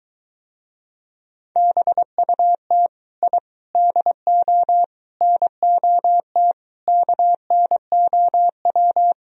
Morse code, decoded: BUT I DO NOT KNOW

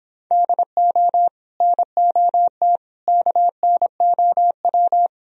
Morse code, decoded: DO NOT KNOW

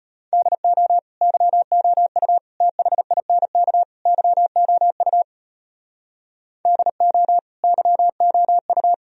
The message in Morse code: DO YOU THINK YOU DO YOU

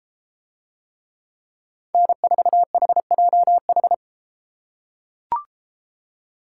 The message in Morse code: N4HJH

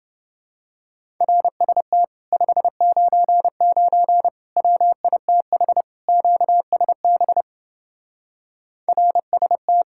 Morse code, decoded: RST 599 WITH QSB RST

